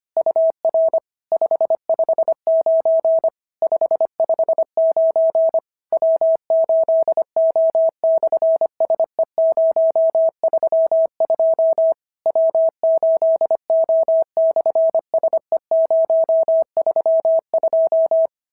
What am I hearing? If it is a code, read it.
Morse code, heard as a keyed tone, about 25 wpm, 650 Hz. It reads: UR 559 559 W8O/SE032 W8O/SE032
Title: SOTA QSO Practice 25wpm